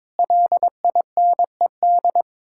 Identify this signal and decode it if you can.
Morse code, heard as a keyed tone, about 22 wpm, 700 Hz. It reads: LINED
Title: Pangram Challenge v2 - 22wpm